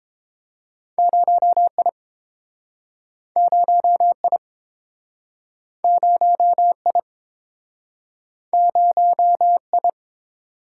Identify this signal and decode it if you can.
Morse code, heard as a keyed tone, about 30 wpm, 700 Hz. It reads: 0 I 0 I 0 I 0 I